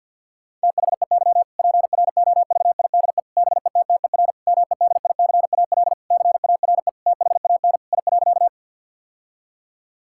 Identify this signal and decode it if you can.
Morse code, heard as a keyed tone, about 50 wpm, 700 Hz. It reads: THEY PROVIDE BETTER MEDICAL CARE THAN I.
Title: Sentences from Top 1000 Words - Review Part 5 - 50wpm